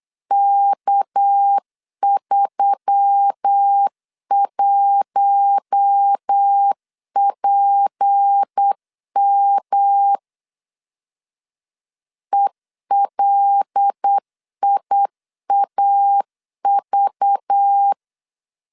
Morse code, heard as a keyed tone, about 8 wpm, 800 Hz. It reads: K31PM ELIAV
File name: LEZIONI / 6 PKX0 / 6 PKX0 8wpm 3341